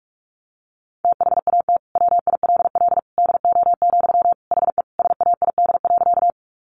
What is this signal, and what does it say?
Morse code, heard as a keyed tone, about 44 wpm, 700 Hz. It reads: THAT WILL DO, HE SAID.
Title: Sentences from Top 100 Words 45wpm